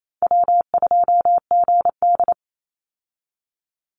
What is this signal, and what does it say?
Morse code, heard as a keyed tone, about 28 wpm, 700 Hz. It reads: W2GD
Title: Call Signs - 28wpm